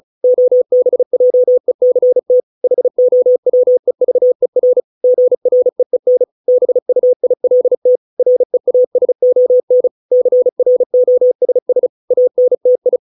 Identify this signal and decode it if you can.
Morse code, heard as a keyed tone, about 34 wpm, 500 Hz. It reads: OBJECT HOWEVER GREEN BUILT REASON CROSS ANTI